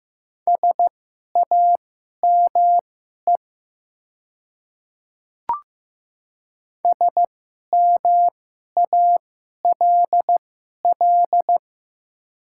Morse code, heard as a keyed tone, about 15 wpm, spaced as for 11 wpm, 700 Hz. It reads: SAME SMALL
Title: 2x Character Spacing - Top 200 Words - 15wpm